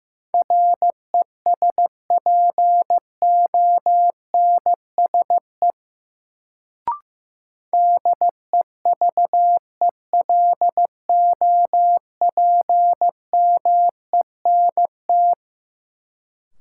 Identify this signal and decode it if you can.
Morse code, heard as a keyed tone, about 15 wpm, 700 Hz. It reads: RESPONSE DEVELOPMENT
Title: Sentences from Top 1400 Words - New Word Drill Alt - 15wpm